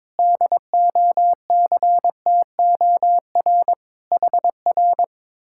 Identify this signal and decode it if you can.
Morse code, heard as a keyed tone, about 22 wpm, 700 Hz. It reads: DOCTOR HR
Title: Speed-Racing Common QSO Phrases - 22wpm